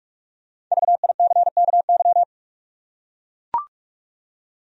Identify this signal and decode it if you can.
Morse code, heard as a keyed tone, about 43 wpm, 700 Hz. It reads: VICKY
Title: USA Common Names - Max 5 Letters - 45wpm